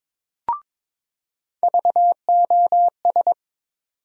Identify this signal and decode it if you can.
Morse code, heard as a keyed tone, about 22 wpm, 700 Hz. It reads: VOS
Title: USA Call Sign Postfixes - 3 Letters 22wpm